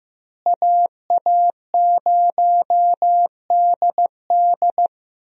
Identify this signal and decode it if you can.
Morse code, heard as a keyed tone, about 16 wpm, 700 Hz. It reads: AA0DD
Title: USA Call Signs - 5 Characters - Repeated 3x - 15wpm